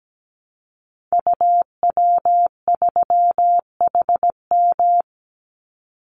Morse code, heard as a keyed tone, about 17 wpm, 700 Hz. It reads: UW3HM
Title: Call Signs - 17wpm